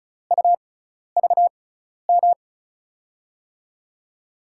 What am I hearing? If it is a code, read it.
Morse code, heard as a keyed tone, about 36 wpm, spaced as for 12 wpm, 700 Hz. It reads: UVM
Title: USA Call Signs - 6 Characters - 6x Character Spacing - 35wpm